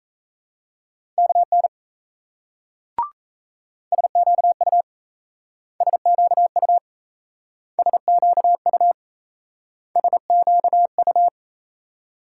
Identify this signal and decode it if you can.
Morse code, heard as a keyed tone, about 38 wpm, 700 Hz. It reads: KN SQU SQU SQU SQU